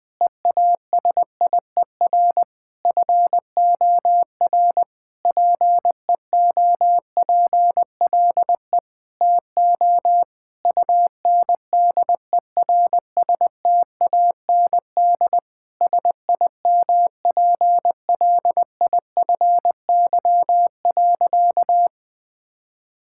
Morse code, heard as a keyed tone, about 20 wpm, 700 Hz. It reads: EASIER FOR PEOPLE TO UNDERSTAND SIMPLIFY.
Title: Sentences with 7-Word Phrases - 20wpm